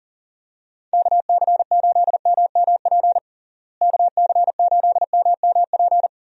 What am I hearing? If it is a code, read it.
Morse code, heard as a keyed tone, about 40 wpm, 700 Hz. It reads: KC8MMP KC8MMP
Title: USA Call Signs - 6 Characters - Repeated 3x - 40wpm